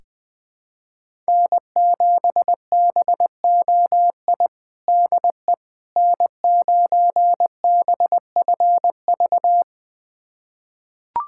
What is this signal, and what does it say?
Morse code, heard as a keyed tone, about 20 wpm, 700 Hz. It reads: N7BOI DE N9BFV
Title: USA Callsign DE Callsign - 5 Characters - 20wpm